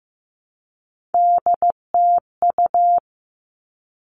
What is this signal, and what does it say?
Morse code, heard as a keyed tone, about 15 wpm, 700 Hz. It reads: DTU
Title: All Cut Numbers - 3-Digit 15wpm